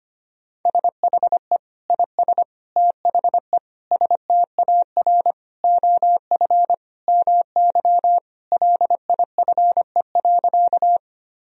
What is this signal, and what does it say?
Morse code, heard as a keyed tone, about 25 wpm, 700 Hz. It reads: SHE IS THE STAR OF MY LIFE.
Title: Sentences from Top 1000 Words - Review Part 2 - 25wpm